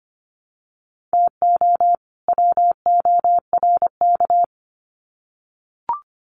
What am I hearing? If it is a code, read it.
Morse code, heard as a keyed tone, about 25 wpm, 700 Hz. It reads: TO WORK
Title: Sets of 2 Words 25wpm